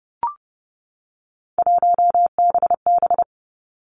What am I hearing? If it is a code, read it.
Morse code, heard as a keyed tone, about 30 wpm, 700 Hz. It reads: 1BB
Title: Three Challenging Letter-Number Combinations 30wpm